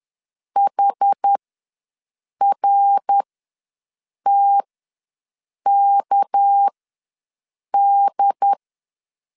Morse code, heard as a keyed tone, about 10 wpm, spaced as for 6 wpm, 800 Hz. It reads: HRTKD
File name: LEZIONI / 6 PKX0 / 6 PK ws 10wpm 3745